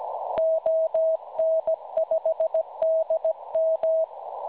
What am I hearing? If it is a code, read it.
Morse code, heard as a keyed tone, about 17 wpm, 640 Hz. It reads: ON5DM
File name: cwronde / websdr recording .6kHz